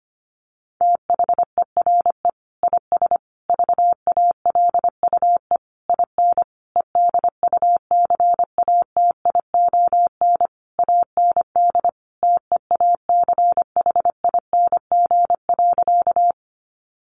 Morse code, heard as a keyed tone, about 25 wpm, 700 Hz. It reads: THERE IS VALUE IN EDUCATION AND TEACHING.
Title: Sentences from Top 1100 Words 25wpm